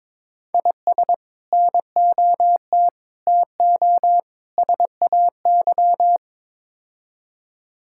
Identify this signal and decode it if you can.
Morse code, heard as a keyed tone, about 22 wpm, 700 Hz. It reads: IS NOT TO SAY
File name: practice / 22wpm / Build A Phrase - 5-Word Phrases - 22wpm